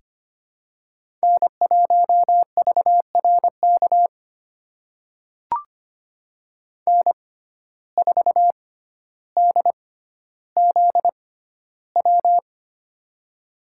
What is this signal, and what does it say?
Morse code, heard as a keyed tone, about 25 wpm, spaced as for 13 wpm, 700 Hz. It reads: N1VRK N4DZW
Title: USA Call Signs - 5 Characters - 6x Character Spacing - 25wpm